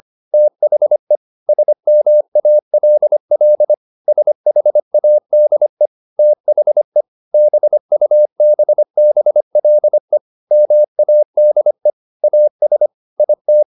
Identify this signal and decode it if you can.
Morse code, heard as a keyed tone, about 25 wpm, 600 Hz. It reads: THE SMALL SHADE THE BUBBLE MADE AS IT
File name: the old man and the sea part 1 25wpm